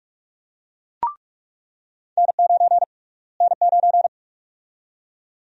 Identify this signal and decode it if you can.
Morse code, heard as a keyed tone, about 43 wpm, 700 Hz. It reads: N9 N9
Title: USA Call Sign Prefixes - Repeated 2x - 45wpm